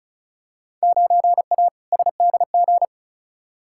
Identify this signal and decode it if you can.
Morse code, heard as a keyed tone, about 36 wpm, 700 Hz. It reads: 9A SDG
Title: Field Day Exchanges Encore - 35wpm